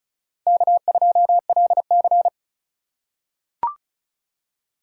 Morse code, heard as a keyed tone, about 36 wpm, 700 Hz. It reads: K2LC
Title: USA Call Signs - 4 Characters - 3x Character Spacing - 35wpm